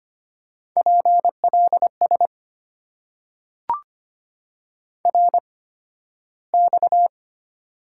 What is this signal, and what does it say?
Morse code, heard as a keyed tone, about 25 wpm, spaced as for 13 wpm, 700 Hz. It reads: PLS RX
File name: Top 90 QSO Elements - 8x Character Spacing - 25wpm